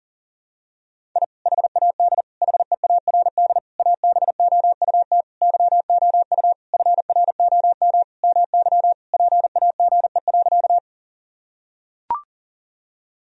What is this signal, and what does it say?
Morse code, heard as a keyed tone, about 40 wpm, 700 Hz. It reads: I HAD HEARD ABOUT YOU FROM MY PAGE.
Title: Sentences from Top 1000 Words - Review Part 1 - 40wpm